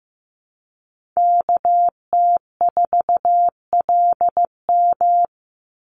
Morse code, heard as a keyed tone, about 15 wpm, 700 Hz. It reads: KT4LM
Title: USA Call Signs 15wpm